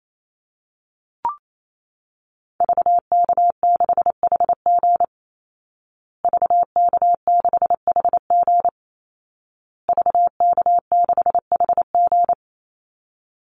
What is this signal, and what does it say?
Morse code, heard as a keyed tone, about 28 wpm, 700 Hz. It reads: VK6HG VK6HG VK6HG
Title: Call Signs Repeated 2x - 28wpm